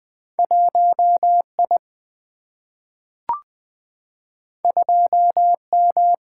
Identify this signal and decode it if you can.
Morse code, heard as a keyed tone, about 20 wpm, 700 Hz. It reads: 1 I 2 M
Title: Single-Letter Single-Number 20wpm